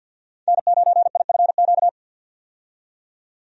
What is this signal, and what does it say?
Morse code, heard as a keyed tone, about 52 wpm, 700 Hz. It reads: N9IFQ
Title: USA Call Signs - 5 Characters - 50wpm